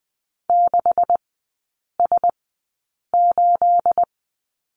Morse code, heard as a keyed tone, about 20 wpm, spaced as for 8 wpm, 690 Hz. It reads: 6S8
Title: 212 Lesson 34e - Number 8 - Three Characters 20wpm